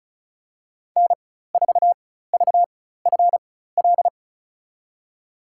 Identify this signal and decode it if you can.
Morse code, heard as a keyed tone, about 36 wpm, spaced as for 16 wpm, 700 Hz. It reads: N4VFL
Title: USA Call Signs - 5 Characters - 4x Character Spacing - 35wpm